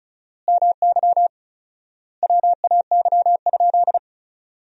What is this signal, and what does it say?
Morse code, heard as a keyed tone, about 36 wpm, 700 Hz. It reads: MY WAY?
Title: 4x Word-Spacing - Sentences from Top 200 Words - 35wpm